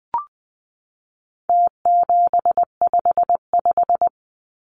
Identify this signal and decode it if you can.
Morse code, heard as a keyed tone, about 20 wpm, 700 Hz. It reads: T755
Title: Numbers - Contest Style - 4-Digit 20wpm